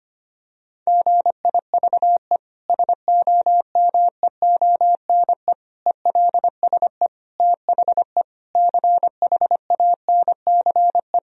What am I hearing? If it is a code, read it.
Morse code, heard as a keyed tone, about 25 wpm, 700 Hz. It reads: GIVE SOMEONE ELSE THE CHANCE
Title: Sentences from Top 1000 Words - Review Part 4 - 25wpm